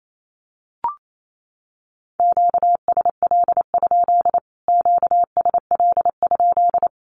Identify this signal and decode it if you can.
Morse code, heard as a keyed tone, about 28 wpm, 700 Hz. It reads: QSL? QSL?
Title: Q-Codes Repeated 3x 28wpm